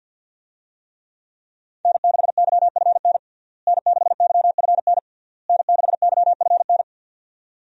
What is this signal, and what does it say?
Morse code, heard as a keyed tone, about 49 wpm, 700 Hz. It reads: N6YFN N6YFN N6YFN
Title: USA Call Signs - 5 Characters - Repeated 3x - 50wpm